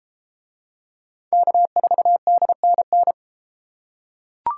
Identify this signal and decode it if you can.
Morse code, heard as a keyed tone, about 32 wpm, 700 Hz. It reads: K4DNN